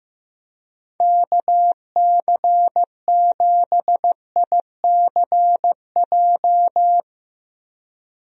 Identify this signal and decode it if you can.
Morse code, heard as a keyed tone, about 16 wpm, 700 Hz. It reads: KC7ICJ
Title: USA Call Signs - 6 Characters - Repeated 3x - 15wpm